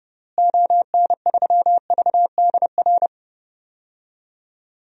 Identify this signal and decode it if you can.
Morse code, heard as a keyed tone, about 32 wpm, 700 Hz. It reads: ON3VDR
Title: DXCC-100 Call Signs - 6 Characters - Repeated 2x - 30wpm